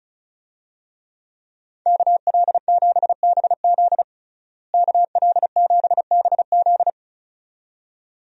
Morse code, heard as a keyed tone, about 36 wpm, 700 Hz. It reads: KL7BZ KL7BZ
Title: DXCC-100 Call Signs - 5 Characters - Repeated 2x - 35wpm